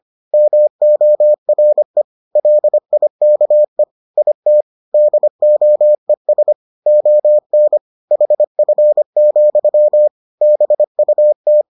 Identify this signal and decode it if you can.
Morse code, heard as a keyed tone, about 25 wpm, 600 Hz. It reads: MORE LIKE IT DOES ON HF, BUT